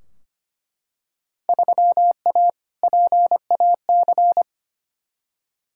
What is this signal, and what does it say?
Morse code, heard as a keyed tone, about 25 wpm, 700 Hz. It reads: 3A PAC
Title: Field Day Exchanges Encore - 25wpm